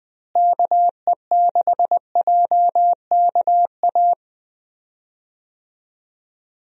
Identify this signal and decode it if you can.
Morse code, heard as a keyed tone, about 21 wpm, 700 Hz. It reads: KE6JKA
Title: USA Call Signs - 6 Characters - Repeated 2x - 20wpm